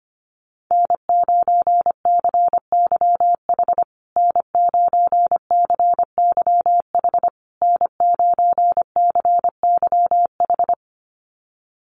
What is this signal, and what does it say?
Morse code, heard as a keyed tone, about 25 wpm, 700 Hz. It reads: N9CYH N9CYH N9CYH
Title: USA Call Signs Repeated 3x 25wpm